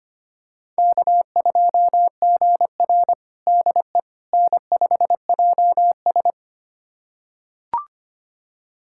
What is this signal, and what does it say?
Morse code, heard as a keyed tone, about 25 wpm, 700 Hz. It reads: K2GR DE N5JS
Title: USA Callsign DE Callsign - 4 Characters - 25wpm